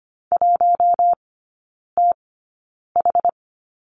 Morse code, heard as a keyed tone, about 25 wpm, spaced as for 9 wpm, 700 Hz. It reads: 1TH
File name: course / 25:10wpm / 093 Lesson 16e - Letter c - Three Characters 25wpm